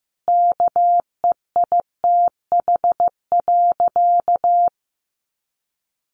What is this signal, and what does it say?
Morse code, heard as a keyed tone, about 14 wpm, 700 Hz. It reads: KEITH.
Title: A Sound Of Thunder 15wpm